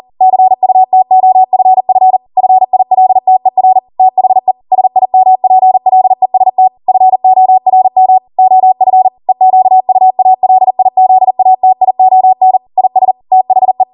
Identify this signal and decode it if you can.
Morse code, heard as a keyed tone, about 40 wpm, 750 Hz. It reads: CUTOFF FILTER THE SIMPLEST FORM OF EQUALIZATION IS THE